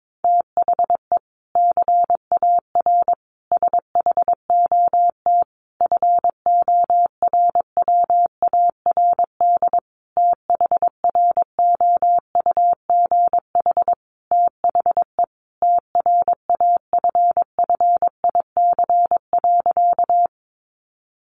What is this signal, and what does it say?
Morse code, heard as a keyed tone, about 22 wpm, 700 Hz. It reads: THE CAR SHOT FORWARD THROUGH THE TRAFFIC.
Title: Sentences from Top 1300 Words 22wpm